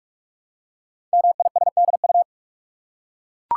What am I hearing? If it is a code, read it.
Morse code, heard as a keyed tone, about 44 wpm, 700 Hz. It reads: MISDU
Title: Code Groups - 5 Letters - 45wpm